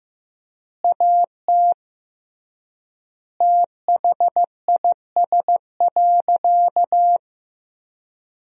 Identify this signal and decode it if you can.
Morse code, heard as a keyed tone, about 15 wpm, 700 Hz. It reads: AT THIS.
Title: 3x Word-Spacing - Sentences from Top 300 Words - 15wpm